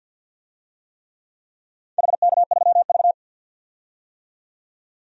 Morse code, heard as a keyed tone, about 52 wpm, 700 Hz. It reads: HK3V
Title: DXCC-100 Call Signs - 4 Characters - 50wpm